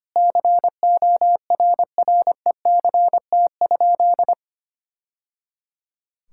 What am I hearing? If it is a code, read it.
Morse code, heard as a keyed tone, about 25 wpm, 700 Hz. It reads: CORRECT?
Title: Sentences from Top 1000 Words - Review Part 5 - 25wpm